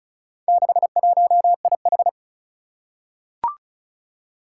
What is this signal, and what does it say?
Morse code, heard as a keyed tone, about 36 wpm, 700 Hz. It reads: 61IH
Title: Four Challenging Letter-Number Combinations 35wpm